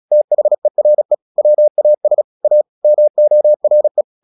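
Morse code, heard as a keyed tone, about 36 wpm, 600 Hz. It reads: THERE WAS A MORE
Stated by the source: Morse practice at 36 wpm, text of Ken Follett's novel Never, from CW med Gnister.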